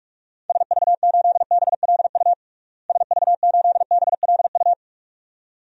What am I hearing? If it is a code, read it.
Morse code, heard as a keyed tone, about 42 wpm, 700 Hz. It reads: SV8BLU SV8BLU
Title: DXCC-100 Call Signs - 6 Characters - Repeated 2x - 45wpm